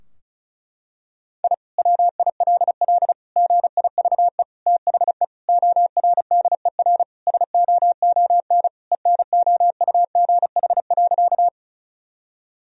Morse code, heard as a keyed tone, about 34 wpm, 700 Hz. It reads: I WILL GIVE THE ORDER SOON ENOUGH.
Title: Sentences from Top 1000 Words - Review Part 2 - 35wpm